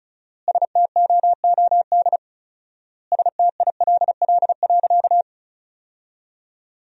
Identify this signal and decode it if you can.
Morse code, heard as a keyed tone, about 36 wpm, 700 Hz. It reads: STOOD STILL.
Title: 4x Word-Spacing - Sentences from Top 500 Words - 35wpm